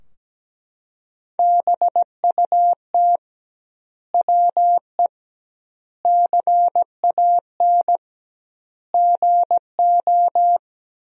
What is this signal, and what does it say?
Morse code, heard as a keyed tone, about 17 wpm, 700 Hz. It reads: BUT WE CAN GO
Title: 2x Word-Spacing - Sentences from Top 100 Words - 17wpm